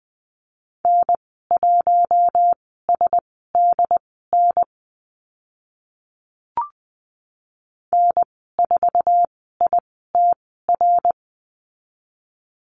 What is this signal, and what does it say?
Morse code, heard as a keyed tone, about 20 wpm, spaced as for 14 wpm, 700 Hz. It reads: N1SDN N4ITR
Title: 082 Lesson 14g - Number 2 - Call Signs 20wpm